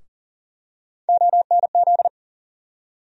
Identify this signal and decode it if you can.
Morse code, heard as a keyed tone, about 40 wpm, 700 Hz. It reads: ONZ
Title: USA Call Sign Postfixes - 3 Letters 40wpm